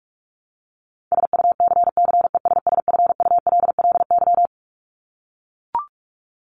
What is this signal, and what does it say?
Morse code, heard as a keyed tone, about 44 wpm, 700 Hz. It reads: SUCCESSFULLY
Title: Top 1501-1750 Words 45wpm